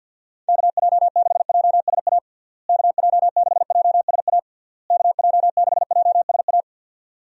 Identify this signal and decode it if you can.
Morse code, heard as a keyed tone, about 51 wpm, 700 Hz. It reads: KJ6JSA KJ6JSA KJ6JSA
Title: USA Call Signs - 6 Characters - Repeated 3x - 50wpm